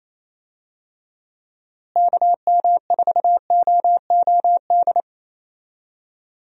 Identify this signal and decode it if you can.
Morse code, heard as a keyed tone, about 28 wpm, 700 Hz. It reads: KM4OOD
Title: USA Call Signs - 6 Characters - Repeated 3x - 28wpm